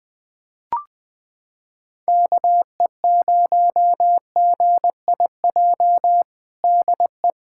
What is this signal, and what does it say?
Morse code, heard as a keyed tone, about 20 wpm, 700 Hz. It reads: KE0GIJ DE
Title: USA Callsign DE Callsign - 6 Characters - 20wpm